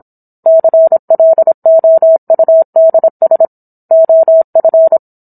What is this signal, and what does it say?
Morse code, heard as a keyed tone, about 27 wpm, 650 Hz. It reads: CLOUDS OF